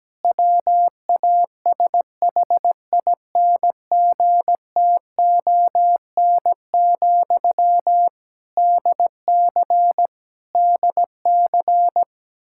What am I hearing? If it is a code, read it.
Morse code, heard as a keyed tone, about 17 wpm, 700 Hz. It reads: WASHINGTON, DC DC
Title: Practice QSO - Exchange 1 - Beginning - 17wpm